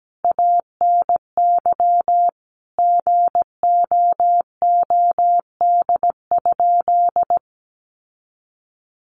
Morse code, heard as a keyed tone, about 17 wpm, 700 Hz. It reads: ANY GOOD?
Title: Sentences from Top 1500 Words 17wpm